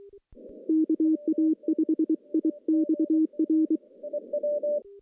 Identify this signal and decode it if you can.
Morse code, heard as a keyed tone, about 23 wpm, 320 Hz. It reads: KA5IXR
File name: contests / CW-SS-k5tr- -1500